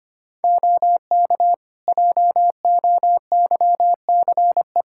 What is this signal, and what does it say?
Morse code, heard as a keyed tone, about 25 wpm, 700 Hz. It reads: OK JOYCE
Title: Practice QSO - Exchange 3 - More Info - 25wpm